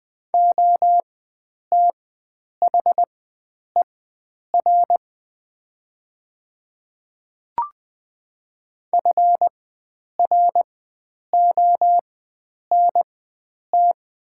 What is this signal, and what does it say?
Morse code, heard as a keyed tone, about 20 wpm, spaced as for 9 wpm, 700 Hz. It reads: OTHER FRONT
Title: 4x Character Spacing - Top 500 Words - 20wpm